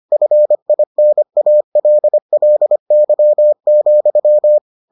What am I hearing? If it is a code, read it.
Morse code, heard as a keyed tone, about 25 wpm, 600 Hz. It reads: FINALLY,